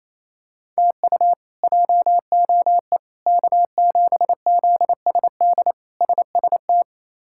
Morse code, heard as a keyed tone, about 28 wpm, 700 Hz. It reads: TU JOE K7ZSD SST
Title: K1USN SST QSO Practice - 28wpm